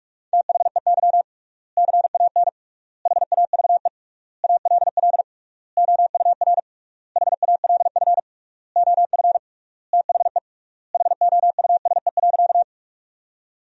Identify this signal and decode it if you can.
Morse code, heard as a keyed tone, about 44 wpm, 700 Hz. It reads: THEY CAN HAVE ALL OUR HALF OF THE HOUSE.
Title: 3x Word-Spacing - Sentences from Top 400 Words - 45wpm